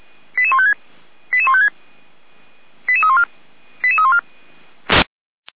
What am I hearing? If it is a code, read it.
Five-tone selective call (ZVEI), heard as quick runs of five tones, tones of about 70 ms. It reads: 88156 88156 88314 88314
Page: /455/SDR/ZVEI/